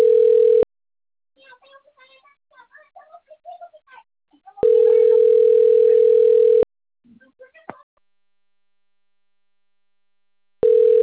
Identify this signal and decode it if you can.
Ringback tone (North America), 440 + 480 Hz.